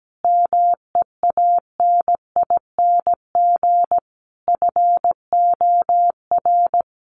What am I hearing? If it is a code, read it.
Morse code, heard as a keyed tone, about 17 wpm, 700 Hz. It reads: MEANING FOR
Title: Sentences from Top 900 Words 17wpm